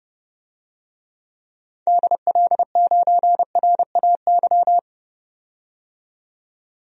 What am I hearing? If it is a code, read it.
Morse code, heard as a keyed tone, about 30 wpm, 700 Hz. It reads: DL9RAY
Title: DXCC-100 Call Signs - 6 Characters - 30wpm